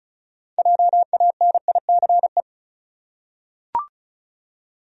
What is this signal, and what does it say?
Morse code, heard as a keyed tone, about 34 wpm, 700 Hz. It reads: JANICE
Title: USA Common Names - 35wpm